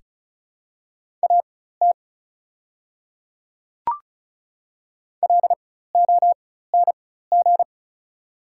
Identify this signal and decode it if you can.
Morse code, heard as a keyed tone, about 36 wpm, spaced as for 16 wpm, 690 Hz. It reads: AT LONG
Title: 4x Character Spacing - Top 100 Words - 35wpm